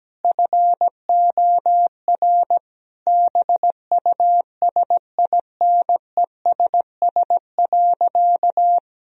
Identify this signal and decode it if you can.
Morse code, heard as a keyed tone, about 17 wpm, 700 Hz. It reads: FOR BUSINESS.
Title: Sentences from Top 1000 Words - Review Part 3 - 17wpm